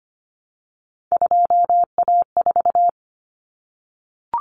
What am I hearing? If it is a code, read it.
Morse code, heard as a keyed tone, about 25 wpm, 700 Hz. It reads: 2A4
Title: Common Cut Numbers - 3-Digit 25wpm